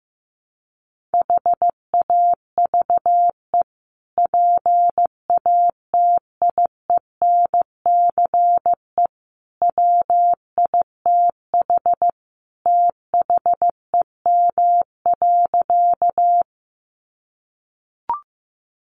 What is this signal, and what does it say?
Morse code, heard as a keyed tone, about 14 wpm, 700 Hz. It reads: HAVE PATIENCE WITH THEM.
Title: Sentences from Top 1300 Words 15wpm